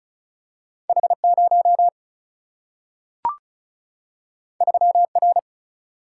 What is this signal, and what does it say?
Morse code, heard as a keyed tone, about 36 wpm, 700 Hz. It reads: H 0 3 R
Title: Single-Letter Single-Number 35wpm